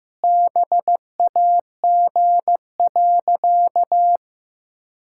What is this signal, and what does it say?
Morse code, heard as a keyed tone, about 15 wpm, 700 Hz. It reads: BAG.
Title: Pangram Challenge v2 - 15wpm